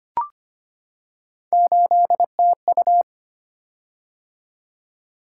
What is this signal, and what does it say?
Morse code, heard as a keyed tone, about 25 wpm, 700 Hz. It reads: 8TU
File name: Three Letter-Number Combinations - 25wpm